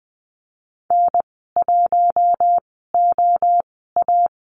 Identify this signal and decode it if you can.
Morse code, heard as a keyed tone, about 20 wpm, spaced as for 14 wpm, 700 Hz. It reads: N1OA
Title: 042 Lesson 8g - Number 1 - Call Signs 20wpm